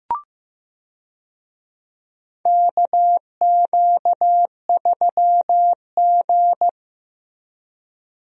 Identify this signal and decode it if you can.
Morse code, heard as a keyed tone, about 15 wpm, 700 Hz. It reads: KQ3G
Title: USA Call Signs - 4 Characters - 15wpm